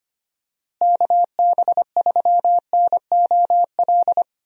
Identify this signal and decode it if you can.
Morse code, heard as a keyed tone, about 25 wpm, 700 Hz. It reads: KB3NOL
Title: USA Call Signs - 6 Characters - Repeated 2x - 25wpm